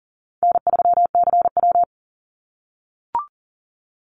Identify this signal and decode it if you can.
Morse code, heard as a keyed tone, about 41 wpm, 700 Hz. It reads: N3CW
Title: Easy Call Signs - 40wpm